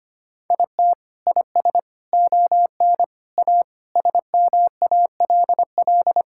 Morse code, heard as a keyed tone, about 25 wpm, 700 Hz. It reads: IT IS ON A SMALL
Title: Sentences from Top 1000 Words - Review Part 5 - 25wpm